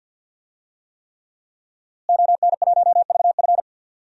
Morse code, heard as a keyed tone, about 46 wpm, 700 Hz. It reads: ON1VF
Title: DXCC-100 Call Signs - 5 Characters - Repeated 3x - 50wpm